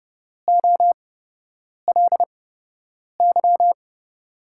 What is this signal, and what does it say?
Morse code, heard as a keyed tone, about 31 wpm, spaced as for 8 wpm, 700 Hz. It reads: OLY
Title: USA Call Signs - 8x Character Spacing - 30wpm